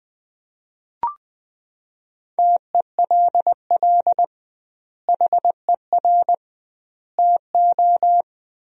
Morse code, heard as a keyed tone, about 20 wpm, 700 Hz. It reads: TELL HER TO
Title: 2x Word-Spacing - Sentences from Top 300 Words - 20wpm